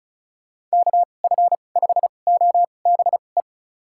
Morse code, heard as a keyed tone, about 36 wpm, spaced as for 25 wpm, 700 Hz. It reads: KF5OBE
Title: USA Call Signs - 6 Characters - 2x Character Spacing - 35wpm